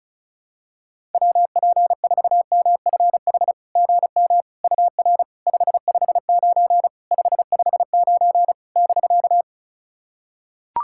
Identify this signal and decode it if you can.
Morse code, heard as a keyed tone, about 36 wpm, 700 Hz. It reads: WP4MFH GM UR 559 559 <BK>
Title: SOTA QSO Practice 35wpm